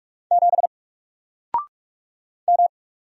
Morse code, heard as a keyed tone, about 44 wpm, 700 Hz. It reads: 7 M
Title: Single Letter-Number Alt 45wpm